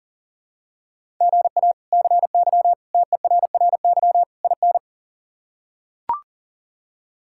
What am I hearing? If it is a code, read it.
Morse code, heard as a keyed tone, about 40 wpm, 700 Hz. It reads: GA CY TERRY IN
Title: K1USN SST QSO Practice - 40wpm